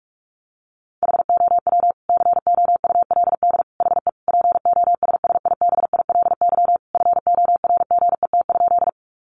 Morse code, heard as a keyed tone, about 44 wpm, 700 Hz. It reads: HOW COULD HE POSSIBILY FORGET?
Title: Sentences from Top 1300 Words 45wpm